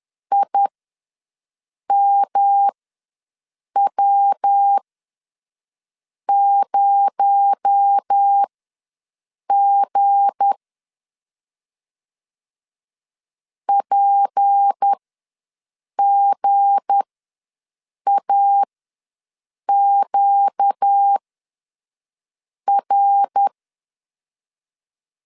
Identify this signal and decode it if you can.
Morse code, heard as a keyed tone, about 10 wpm, spaced as for 5 wpm, 800 Hz. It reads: IMW0G PGAQR
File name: LEZIONI / 7 GF9Z / 7 G ws 10wpm 3843